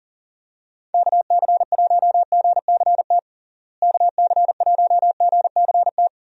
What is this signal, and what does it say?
Morse code, heard as a keyed tone, about 40 wpm, 700 Hz. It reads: KC1GCT KC1GCT
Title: USA Call Signs - 6 Characters - Repeated 3x - 40wpm